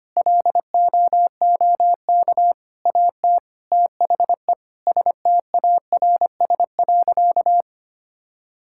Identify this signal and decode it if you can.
Morse code, heard as a keyed tone, about 25 wpm, 700 Hz. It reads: LOOK AT THE STARS.
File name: practice / 25wpm / Sentences with 3-Word Phrases - 25wpm